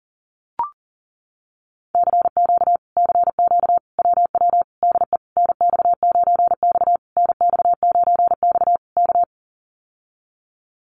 Morse code, heard as a keyed tone, about 37 wpm, 700 Hz. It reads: CQ CQ WW DE NX9X NX9X K
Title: Common QSO Phrases Encore 40wpm